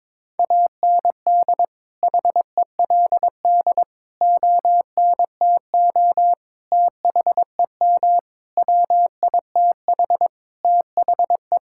Morse code, heard as a keyed tone, about 22 wpm, 700 Hz. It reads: AND HELD ONTO THEM WITH THE
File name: Sentences with 9-Word Phrases - 22wpm